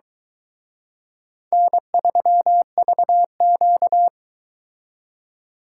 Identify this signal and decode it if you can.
Morse code, heard as a keyed tone, about 22 wpm, 700 Hz. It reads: N3VQ